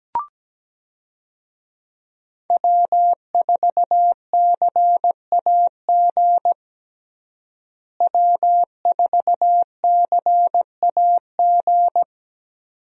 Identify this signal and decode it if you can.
Morse code, heard as a keyed tone, about 17 wpm, 700 Hz. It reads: W4CAG W4CAG
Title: USA Call Signs - 5 Characters - Repeated 2x - 17wpm